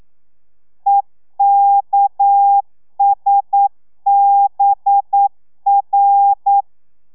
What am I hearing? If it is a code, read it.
Morse code, heard as a keyed tone, about 9 wpm, 800 Hz. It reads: EKSBR
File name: telegrafia / i2tfj / Lez6 40